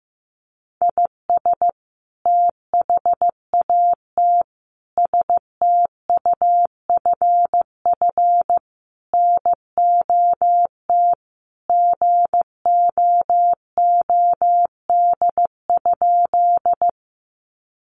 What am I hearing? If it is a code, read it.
Morse code, heard as a keyed tone, about 15 wpm, 700 Hz. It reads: IS THAT STUFF NOT GOOD?
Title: Sentences from Top 1400 Words 15wpm